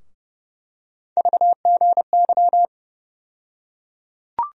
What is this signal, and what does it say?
Morse code, heard as a keyed tone, about 30 wpm, 700 Hz. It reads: VGY
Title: USA Call Sign Postfixes - 3 Letters 30wpm